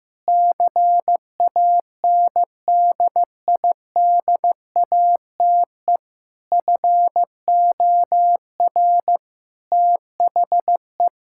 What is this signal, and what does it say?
Morse code, heard as a keyed tone, about 16 wpm, 700 Hz. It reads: CANDIDATE FOR THE
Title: Sentences from Top 1600 Words 15wpm